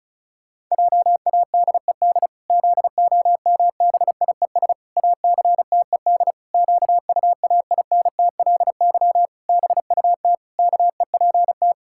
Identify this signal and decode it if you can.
Morse code, heard as a keyed tone, about 34 wpm, 700 Hz. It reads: JADED ZOMBIES ACTED QUAINTLY BUT KEPT
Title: Pangram Challenge v2 - 35wpm